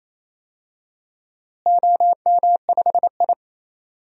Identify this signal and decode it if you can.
Morse code, heard as a keyed tone, about 27 wpm, 700 Hz. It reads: OM5I
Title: DXCC-100 Call Signs - 4 Characters - Repeated 3x - 28wpm